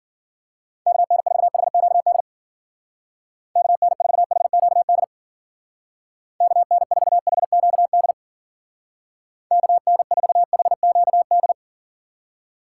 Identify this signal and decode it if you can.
Morse code, heard as a keyed tone, about 52 wpm, 700 Hz. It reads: KN4HQD KN4HQD KN4HQD KN4HQD